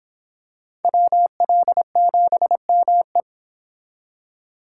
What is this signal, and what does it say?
Morse code, heard as a keyed tone, about 26 wpm, 700 Hz. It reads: WL7ME